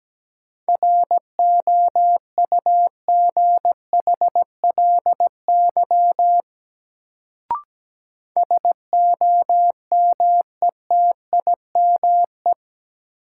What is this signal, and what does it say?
Morse code, heard as a keyed tone, about 17 wpm, 700 Hz. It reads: ROUGHLY SOMETIME
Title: Sentences from Top 1500 Words - New Word Drill Alt - 17wpm